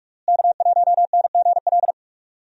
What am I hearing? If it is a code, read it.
Morse code, heard as a keyed tone, about 46 wpm, 700 Hz. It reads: K1NGL
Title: USA Call Signs - 5 Characters - Repeated 3x - 45wpm